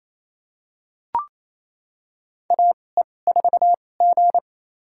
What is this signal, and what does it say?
Morse code, heard as a keyed tone, about 28 wpm, spaced as for 20 wpm, 700 Hz. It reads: AE4G
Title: USA Call Signs - 2x Character Spacing - 28wpm